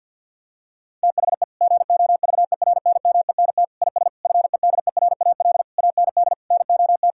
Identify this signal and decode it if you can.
Morse code, heard as a keyed tone, about 50 wpm, 700 Hz. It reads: THE GOVERNMENT IS FEDERAL AND NOT
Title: Sentences from Top 1000 Words - Review Part 5 - 50wpm